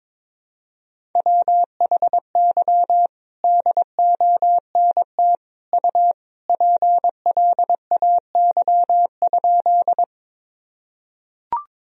Morse code, heard as a keyed tone, about 22 wpm, 700 Hz. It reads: WHY DONT U PLAY?
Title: Sentences from Top 300 Words and Abbreviations 22wpm